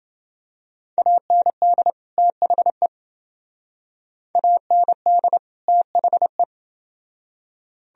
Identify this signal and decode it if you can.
Morse code, heard as a keyed tone, about 28 wpm, 700 Hz. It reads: AND THE AND THE